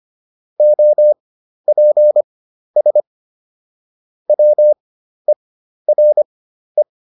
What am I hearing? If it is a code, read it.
Morse code, heard as a keyed tone, about 25 wpm, spaced as for 12 wpm, 600 Hz. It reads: OPS WERE